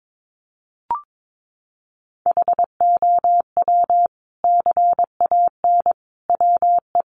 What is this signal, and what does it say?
Morse code, heard as a keyed tone, about 22 wpm, 700 Hz. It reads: HOW CAN WE
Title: Sentences from Top 1100 Words 22wpm